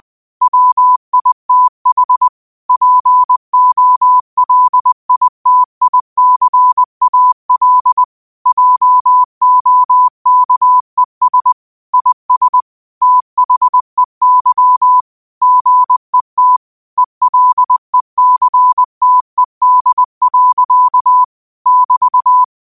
Morse code, heard as a keyed tone, about 20 wpm, 1000 Hz. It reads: WITH POLITICAL JOKES IS THEY GET ELECTED. <BT>
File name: qotdfiles / 20wpm